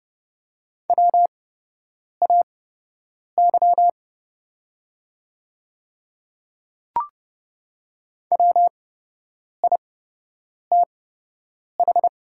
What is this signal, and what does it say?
Morse code, heard as a keyed tone, about 29 wpm, spaced as for 8 wpm, 700 Hz. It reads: WAY WITH
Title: 8x Character Spacing - Top 200 Words - 30wpm